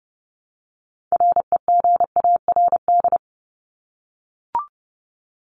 Morse code, heard as a keyed tone, about 30 wpm, 700 Hz. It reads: REGARD
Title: ED Suffix - Top 100 Words with Base Words 30wpm